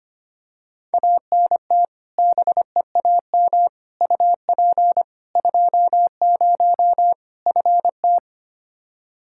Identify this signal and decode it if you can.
Morse code, heard as a keyed tone, about 25 wpm, 700 Hz. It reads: ANT BEAM UP 20 FT
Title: K1USN SST QSO Practice 25wpm